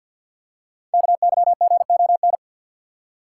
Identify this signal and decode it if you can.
Morse code, heard as a keyed tone, about 52 wpm, 700 Hz. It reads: KYGON
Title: Code Groups - 5 Letters - 50wpm